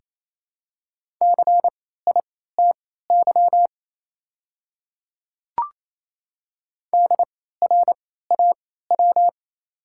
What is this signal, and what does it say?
Morse code, heard as a keyed tone, about 28 wpm, spaced as for 16 wpm, 690 Hz. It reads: CITY DRAW
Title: 3x Character Spacing - Top 500 Words - 28wpm